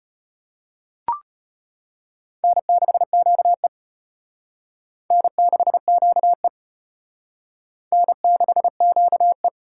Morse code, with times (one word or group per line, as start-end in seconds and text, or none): 2.4-3.7 N6QE
5.1-6.5 N6QE
7.9-9.5 N6QE